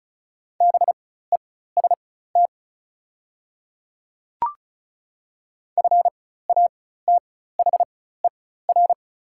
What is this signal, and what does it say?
Morse code, heard as a keyed tone, about 36 wpm, spaced as for 16 wpm, 700 Hz. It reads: BEST FATHER
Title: 4x Character Spacing - Top 500 Words - 35wpm